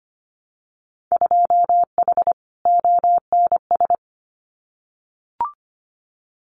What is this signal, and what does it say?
Morse code, heard as a keyed tone, about 25 wpm, 700 Hz. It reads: 2H ONS
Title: Winter Field Day Exchanges Encore 25wpm